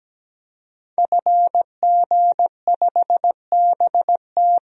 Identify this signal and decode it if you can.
Morse code, heard as a keyed tone, about 17 wpm, 700 Hz. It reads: FG5BT
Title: DXCC-100 Call Signs - 5 Characters - 17wpm